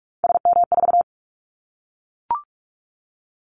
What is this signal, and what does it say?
Morse code, heard as a keyed tone, about 44 wpm, 700 Hz. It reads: SM4
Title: Callsign Prefixes for Top 25 DXCC - 45wpm